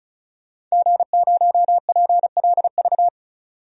Morse code, heard as a keyed tone, about 34 wpm, 700 Hz. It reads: G0PLV
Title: DXCC-100 Call Signs - 5 Characters - 35wpm